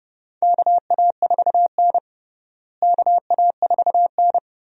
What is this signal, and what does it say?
Morse code, heard as a keyed tone, about 31 wpm, 700 Hz. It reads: KA4N KA4N
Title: USA Call Signs - 4 Characters - Repeated 2x - 30wpm